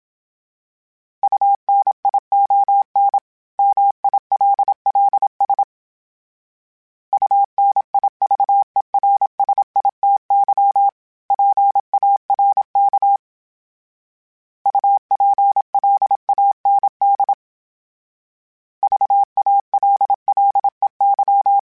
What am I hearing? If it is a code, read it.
Morse code, heard as a keyed tone, about 27 wpm, 800 Hz. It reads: UNION MILLS UNIVERSITY PARK UPLAND VALLEY
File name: fromSMs / FromSMs mp3s / Cities@25